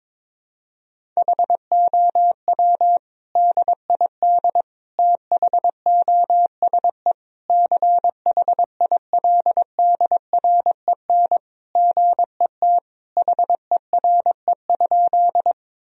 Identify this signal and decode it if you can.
Morse code, heard as a keyed tone, about 22 wpm, 700 Hz. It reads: HOW DID THOSE CHILDREN GET HERE?
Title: Sentences from Top 1000 Words - Review Part 1 - 22wpm